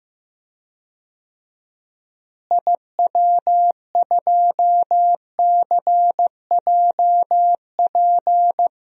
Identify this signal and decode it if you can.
Morse code, heard as a keyed tone, about 16 wpm, 700 Hz. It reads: IW2CJP
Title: DXCC-100 Call Signs - 6 Characters - Repeated 3x - 15wpm